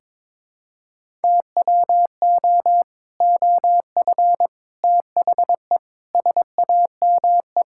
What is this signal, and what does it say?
Morse code, heard as a keyed tone, about 22 wpm, 700 Hz. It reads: TWO OF THE SAME
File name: practice / 22wpm / Sentences with 3-Word Phrases - 22wpm